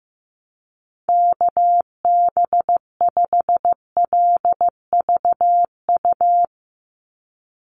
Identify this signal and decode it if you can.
Morse code, heard as a keyed tone, about 15 wpm, 700 Hz. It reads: KB5LVU
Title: USA Call Signs 15wpm